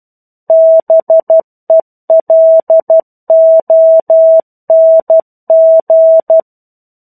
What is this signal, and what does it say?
Morse code, heard as a keyed tone, about 12 wpm, 650 Hz. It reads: BELONG
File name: 01-Chapter-01 12 WPM